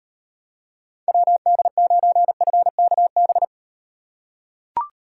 Morse code, heard as a keyed tone, about 34 wpm, 700 Hz. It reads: WD9FKB